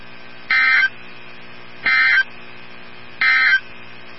Emergency Alert System (SAME) tones, heard as harsh buzzing bursts. It is heard from 0.4 to 3.7 s.
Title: NOAA Weather Radio 162.550